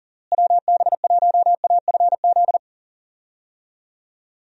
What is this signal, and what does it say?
Morse code, heard as a keyed tone, about 40 wpm, 700 Hz. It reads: WB1AFZ
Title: USA Call Signs - 6 Characters - Repeated 3x - 40wpm